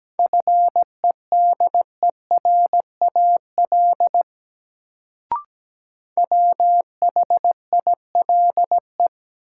Morse code, heard as a keyed tone, about 17 wpm, 700 Hz. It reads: FEDERAL WHILE
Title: Sets of 3 Words - New Word Drill - 17wpm